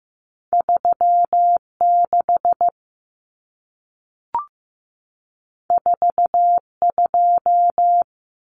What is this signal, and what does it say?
Morse code, heard as a keyed tone, about 15 wpm, 700 Hz. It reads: 36 42
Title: Common Cut Numbers - 2-Digit 15wpm